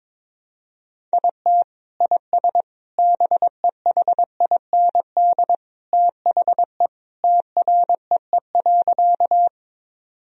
Morse code, heard as a keyed tone, about 22 wpm, 700 Hz. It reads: IT IS BEHIND THE TREE.
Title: Sentences from Top 1000 Words - Review Part 1 - 22wpm